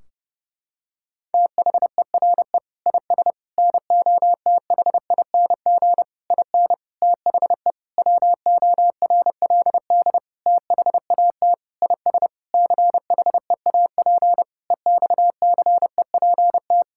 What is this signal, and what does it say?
Morse code, heard as a keyed tone, about 30 wpm, 700 Hz. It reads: THERE IS NOTHING IN THE WORLD THAT IS CHEAP EXCEPT
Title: Sentences from Top 1600 Words 30wpm